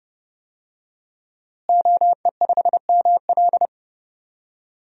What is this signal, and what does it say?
Morse code, heard as a keyed tone, about 32 wpm, 700 Hz. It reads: OE5ML
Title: DXCC-100 Call Signs - 5 Characters - Repeated 3x - 30wpm